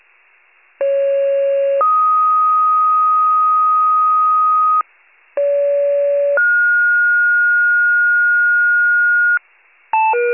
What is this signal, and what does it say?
Two-tone page, heard as a one-second tone, then a three-second tone, about 550 then 1250 Hz, about 570 then 1430 Hz, with noise. It is heard from 0.8 to 9.4 s.
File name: wcalert